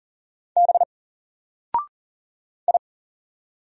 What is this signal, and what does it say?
Morse code, heard as a keyed tone, about 40 wpm, 700 Hz. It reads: B I
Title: Single Character (Letter:Number:Punctuation:Prosign) 40wpm